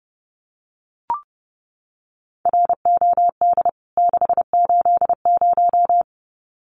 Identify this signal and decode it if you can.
Morse code, heard as a keyed tone, about 30 wpm, 700 Hz. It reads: ROD 680
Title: CWT Exchange 30wpm